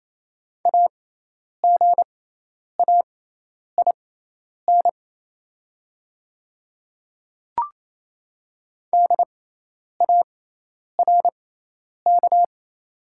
Morse code, heard as a keyed tone, about 28 wpm, spaced as for 10 wpm, 700 Hz. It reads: AGAIN DARK
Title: 6x Character Spacing - Top 500 Words - 28wpm